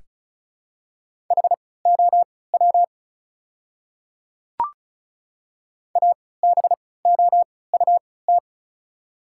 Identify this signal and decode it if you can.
Morse code, heard as a keyed tone, about 36 wpm, spaced as for 20 wpm, 690 Hz. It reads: HOW ABOUT
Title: 3x Character Spacing - Top 100 Words - 35wpm